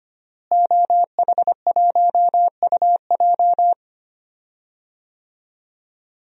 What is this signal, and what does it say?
Morse code, heard as a keyed tone, about 25 wpm, 700 Hz. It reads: OH1UJ
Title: DXCC-100 Call Signs - 5 Characters - Repeated 3x - 25wpm